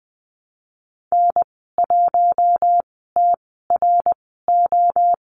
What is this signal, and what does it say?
Morse code, heard as a keyed tone, about 20 wpm, spaced as for 15 wpm, 700 Hz. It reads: N1TRO
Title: 082 Lesson 14g - Number 2 - Call Signs 20wpm